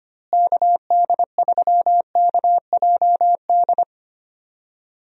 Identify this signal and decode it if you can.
Morse code, heard as a keyed tone, about 25 wpm, 700 Hz. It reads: KD3KJD
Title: USA Call Signs - 6 Characters - Repeated 2x - 25wpm